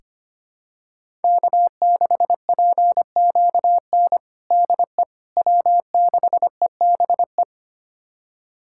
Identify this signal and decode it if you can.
Morse code, heard as a keyed tone, about 25 wpm, 700 Hz. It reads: K6PQN DE W6EBE
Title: USA Callsign DE Callsign - 5 Characters - 25wpm